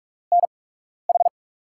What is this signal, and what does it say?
Morse code, heard as a keyed tone, about 42 wpm, spaced as for 12 wpm, 700 Hz. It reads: NH
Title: USA Call Signs - 5 Characters - 8x Character Spacing - 45wpm